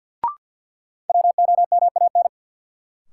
Morse code, heard as a keyed tone, about 53 wpm, 700 Hz. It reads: WOMAN
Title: Sentences from Top 1000 Words - Review Part 2 - New Word Drill - 50wpm